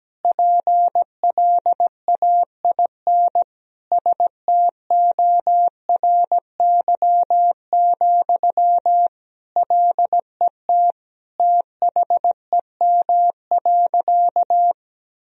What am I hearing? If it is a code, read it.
Morse code, heard as a keyed tone, about 17 wpm, 700 Hz. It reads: PLAIN STORY, LET THEM.
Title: Sentences from Top 1000 Words - Review Part 2 - 17wpm